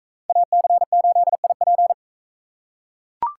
Morse code, heard as a keyed tone, about 42 wpm, 700 Hz. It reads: AC8IP